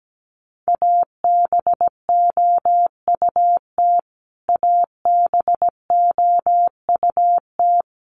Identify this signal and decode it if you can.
Morse code, heard as a keyed tone, about 17 wpm, 700 Hz. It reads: ABOUT ABOUT
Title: Top 200 Words Repeated 17wpm